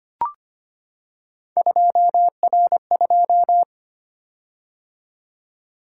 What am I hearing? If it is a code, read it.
Morse code, heard as a keyed tone, about 25 wpm, 700 Hz. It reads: 2R2
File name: Three Letter-Number Combinations - 25wpm